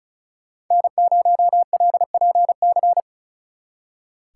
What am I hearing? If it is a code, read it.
Morse code, heard as a keyed tone, about 43 wpm, 700 Hz. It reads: N0LPC
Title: Speed Racing - USA 1x3 Call Signs 35wpm